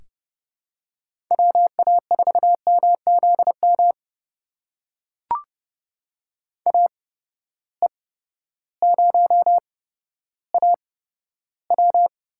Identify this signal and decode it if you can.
Morse code, heard as a keyed tone, about 30 wpm, spaced as for 14 wpm, 700 Hz. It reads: WA4MZM AE0AW
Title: USA Call Signs - 8x Character Spacing - 30wpm